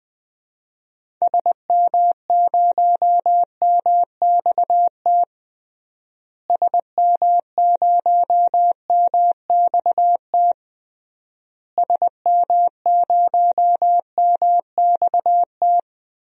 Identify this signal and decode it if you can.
Morse code, heard as a keyed tone, about 21 wpm, 700 Hz. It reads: SM0MXT SM0MXT SM0MXT
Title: DXCC-100 Call Signs - 6 Characters - Repeated 3x - 20wpm